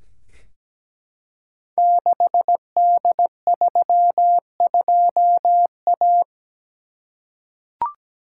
Morse code, heard as a keyed tone, about 17 wpm, 700 Hz. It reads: 6D32A
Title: Code Groups - 5 Letter:Numbers - 17wpm